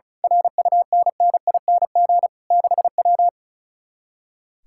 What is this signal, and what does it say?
Morse code, heard as a keyed tone, about 33 wpm, 700 Hz. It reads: RUNNING 6W
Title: Speed-Racing Common QSO Phrases - 35wpm